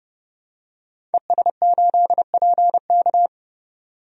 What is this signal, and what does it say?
Morse code, heard as a keyed tone, about 30 wpm, 700 Hz. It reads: ES8PK
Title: DXCC-100 Call Signs - 5 Characters - 30wpm